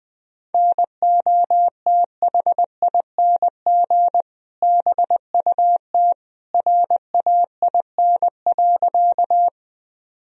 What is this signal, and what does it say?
Morse code, heard as a keyed tone, about 20 wpm, 700 Hz. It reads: NOTHING BUT RAIN.
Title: Sentences from Top 1000 Words - Review Part 1 - 20wpm